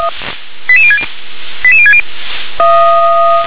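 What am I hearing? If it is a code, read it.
Five-tone selective call (ZVEI), heard as quick runs of five tones, tones of about 70 ms. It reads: 88060 88060
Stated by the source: Five-tone ZVEI selective call received on an SDR.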